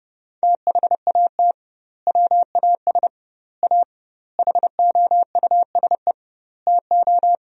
Morse code, heard as a keyed tone, about 30 wpm, 700 Hz. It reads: THAT WAS A HOUSE TO
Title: 2x Word-Spacing - Sentences from Top 200 Words - 30wpm